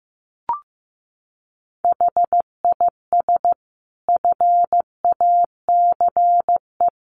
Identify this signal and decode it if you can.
Morse code, heard as a keyed tone, about 15 wpm, 700 Hz. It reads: HIS FACE
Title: Sentences from Top 1200 Words 15wpm